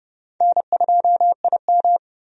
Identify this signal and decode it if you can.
Morse code, heard as a keyed tone, about 31 wpm, 700 Hz. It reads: N2IM
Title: USA Call Signs - 4 Characters - 30wpm